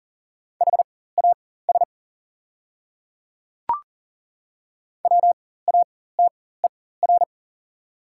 Morse code, heard as a keyed tone, about 40 wpm, spaced as for 19 wpm, 690 Hz. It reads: HAS WATER
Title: 4x Character Spacing - Top 100 Words - 40wpm